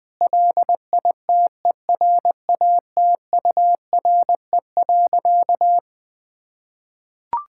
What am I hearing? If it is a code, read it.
Morse code, heard as a keyed tone, about 21 wpm, 700 Hz. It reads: LITERATURE.
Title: Sentences from Top 1000 Words - Review Part 6 - 20wpm